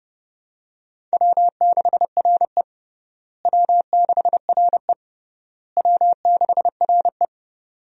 Morse code, heard as a keyed tone, about 30 wpm, 700 Hz. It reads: W6RE W6RE W6RE
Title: USA Call Signs - 4 Characters - Repeated 3x - 30wpm